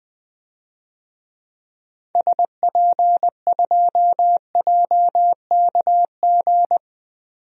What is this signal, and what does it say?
Morse code, heard as a keyed tone, about 20 wpm, 700 Hz. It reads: SP2JKG
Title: DXCC-100 Call Signs - 6 Characters - 20wpm